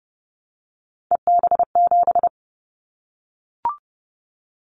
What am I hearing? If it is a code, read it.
Morse code, heard as a keyed tone, about 29 wpm, 700 Hz. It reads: EB7
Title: Callsign Prefixes for Top 25 DXCC - Repeated 3x - 30wpm